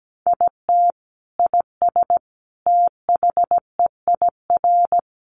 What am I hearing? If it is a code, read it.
Morse code, heard as a keyed tone, about 17 wpm, 700 Hz. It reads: IT IS THEIR
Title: Sentences from Top 1000 Words 17wpm